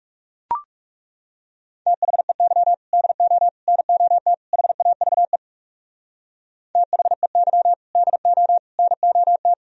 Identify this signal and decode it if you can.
Morse code, heard as a keyed tone, about 39 wpm, 700 Hz. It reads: THEY DO NOT HAVE THEY DO NOT